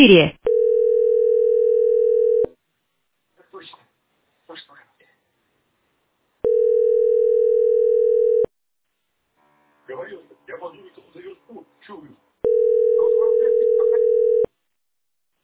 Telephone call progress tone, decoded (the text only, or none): ringback tone